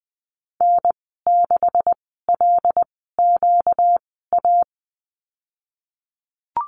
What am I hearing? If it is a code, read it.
Morse code, heard as a keyed tone, about 20 wpm, spaced as for 14 wpm, 700 Hz. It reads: N6LQA
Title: 238 Lesson 38g - Letter q - Call Signs 20wpm